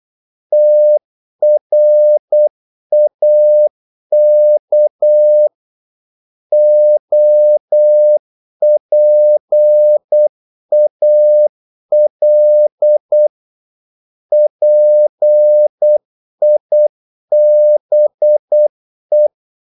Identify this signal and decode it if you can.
Morse code, heard as a keyed tone, about 8 wpm, 600 Hz. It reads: TRAK OPAL PIBE
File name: Korte ord DK 8wpm